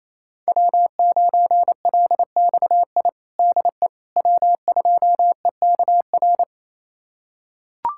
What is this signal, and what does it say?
Morse code, heard as a keyed tone, about 28 wpm, 700 Hz. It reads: W9LXI DE W2EKR
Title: USA Callsign DE Callsign - 5 Characters - 28wpm